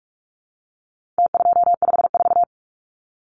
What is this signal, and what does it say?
Morse code, heard as a keyed tone, about 44 wpm, 700 Hz. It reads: T254
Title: Numbers - Contest Style - 4-Digit 45wpm